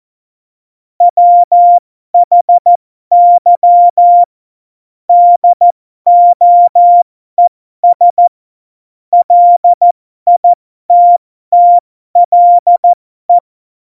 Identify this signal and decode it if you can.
Morse code, heard as a keyed tone, about 14 wpm, spaced as for 12 wpm, 700 Hz. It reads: WHY DOES LITTLE
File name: Morsekurs / Witze